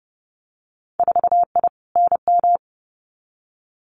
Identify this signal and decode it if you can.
Morse code, heard as a keyed tone, about 30 wpm, 700 Hz. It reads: 4I NM
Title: Winter Field Day Exchanges Encore 30wpm